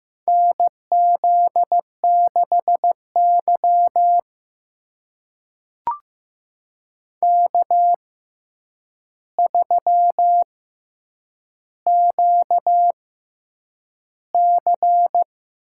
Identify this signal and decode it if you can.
Morse code, heard as a keyed tone, about 15 wpm, spaced as for 8 wpm, 700 Hz. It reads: NZ6Y K3QC
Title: USA Call Signs - 4 Characters - 6x Character Spacing - 15wpm